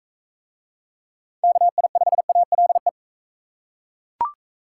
Morse code, heard as a keyed tone, about 40 wpm, 700 Hz. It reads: KI5ALE